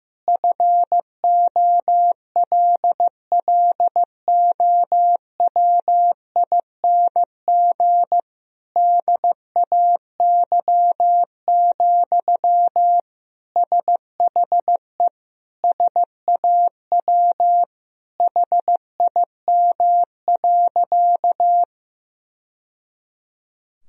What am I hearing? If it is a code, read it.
Morse code, heard as a keyed tone, about 15 wpm, 700 Hz. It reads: FOLLOWING DAY, SHE SAW HIM.
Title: Sentences from Top 1000 Words - Review Part 4 - 15wpm